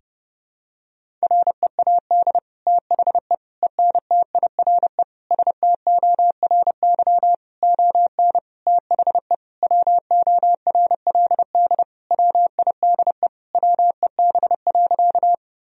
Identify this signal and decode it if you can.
Morse code, heard as a keyed tone, about 30 wpm, 700 Hz. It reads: READ THE ENTIRE STORY ON THE WORLD WIDE WEB.
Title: Sentences from Top 1000 Words - Review Part 4 - 30wpm